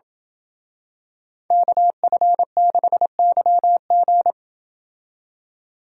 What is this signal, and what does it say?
Morse code, heard as a keyed tone, about 27 wpm, 700 Hz. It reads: KF6YG